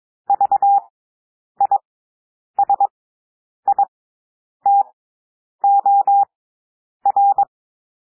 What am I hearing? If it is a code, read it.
Morse code, heard as a keyed tone, about 22 wpm, spaced as for 9 wpm, 800 Hz. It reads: VISITOR
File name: cwmp3 / W2 SOUND WORDS 22 wpm